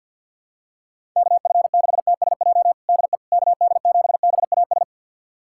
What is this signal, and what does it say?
Morse code, heard as a keyed tone, about 52 wpm, 700 Hz. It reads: KF6TSJ DE KD7BAS
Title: USA Callsign DE Callsign - 6 Characters - 50wpm